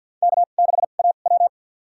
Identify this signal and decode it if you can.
Morse code, heard as a keyed tone, about 53 wpm, spaced as for 37 wpm, 700 Hz. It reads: K6AW
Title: USA Call Signs - 4 Characters - 2x Character Spacing - 50wpm